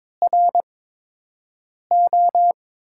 Morse code, heard as a keyed tone, about 22 wpm, 700 Hz. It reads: R O
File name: Top 90 QSO Elements - 8x Character Spacing - 22wpm